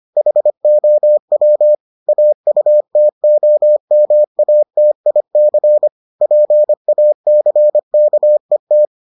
Morse code, heard as a keyed tone, about 25 wpm, 600 Hz. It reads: HOW AUTOMATIC PACKET